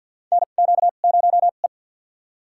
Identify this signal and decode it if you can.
Morse code, heard as a keyed tone, about 51 wpm, spaced as for 36 wpm, 700 Hz. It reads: NQ0E
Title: USA Call Signs - 4 Characters - 2x Character Spacing - 50wpm